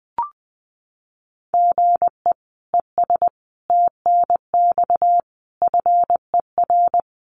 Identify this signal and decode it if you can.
Morse code, heard as a keyed tone, about 20 wpm, 700 Hz. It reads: GE ES TNX FER
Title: Common QSO Phrases Encore 20wpm